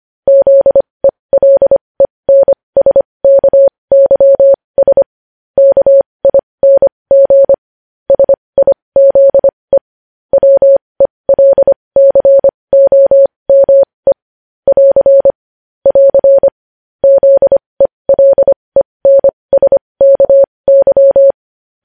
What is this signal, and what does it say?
Morse code, heard as a keyed tone, about 25 wpm, spaced as for 20 wpm, 550 Hz. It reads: ZELENSKYS KING SIZE WELCOME <AR> <AR> ZELENSKY
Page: news-headlines-in-morse-code-at-20-wpm